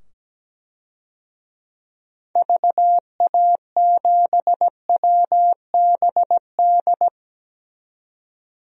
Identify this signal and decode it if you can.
Morse code, heard as a keyed tone, about 17 wpm, 700 Hz. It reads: VA7WBD
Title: DXCC-100 Call Signs - 6 Characters - 17wpm